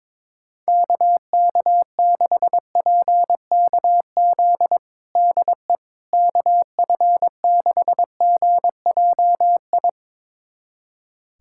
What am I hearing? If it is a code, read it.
Morse code, heard as a keyed tone, about 22 wpm, 700 Hz. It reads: KK6PKZ DE KF6GJI
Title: USA Callsign DE Callsign 22wpm